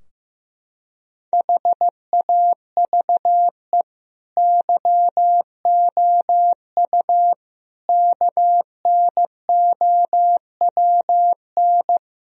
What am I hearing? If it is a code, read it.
Morse code, heard as a keyed tone, about 16 wpm, 700 Hz. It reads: HAVE YOU KNOWN
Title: Sentences from Top 1000 Words - Review Part 3 - 15wpm